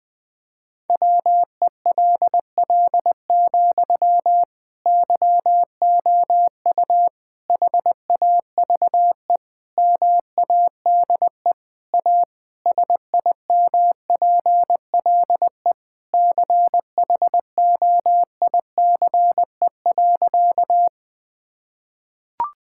Morse code, heard as a keyed tone, about 20 wpm, 700 Hz. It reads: WELL, YOU HAVE MADE A SIMPLE CHOICE.
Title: Sentences from Top 1000 Words - Review Part 6 - 20wpm